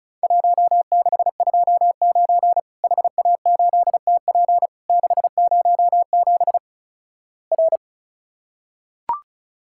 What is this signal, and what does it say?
Morse code, heard as a keyed tone, about 36 wpm, 700 Hz. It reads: 1629 HA8TP 607 R
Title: Worked All Europe DX Contest - QTC Exchange - 35wpm